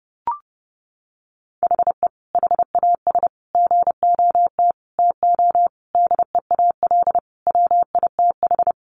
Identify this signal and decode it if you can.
Morse code, heard as a keyed tone, about 30 wpm, 700 Hz. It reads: HE HAS GOT TO DEAL WITH
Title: Sentences from Top 800 Words 30wpm